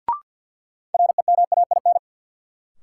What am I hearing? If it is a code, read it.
Morse code, heard as a keyed tone, about 46 wpm, 700 Hz. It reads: REMAIN
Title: Sentences from Top 1200 Words - New Word Drill Alt - 50wpm